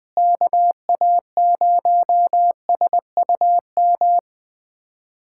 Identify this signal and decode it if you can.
Morse code, heard as a keyed tone, about 21 wpm, 700 Hz. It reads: KA0SUM
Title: USA Callsign DE Callsign - 6 Characters - 20wpm